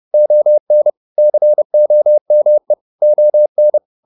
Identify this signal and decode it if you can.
Morse code, heard as a keyed tone, about 30 wpm, 600 Hz. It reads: ON COME ON